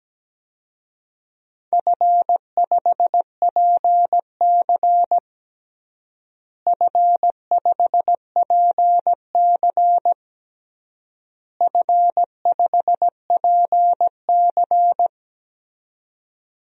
Morse code, heard as a keyed tone, about 17 wpm, 700 Hz. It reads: F5PC F5PC F5PC
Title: DXCC-100 Call Signs - 4 Characters - Repeated 3x - 17wpm